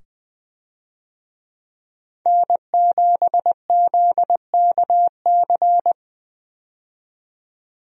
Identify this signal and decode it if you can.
Morse code, heard as a keyed tone, about 21 wpm, 700 Hz. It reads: N7ZKC
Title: USA Call Signs - 5 Characters - 20wpm